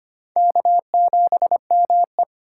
Morse code, heard as a keyed tone, about 25 wpm, 700 Hz. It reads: K7ME
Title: USA Call Signs - 4 Characters - 6x Character Spacing - 25wpm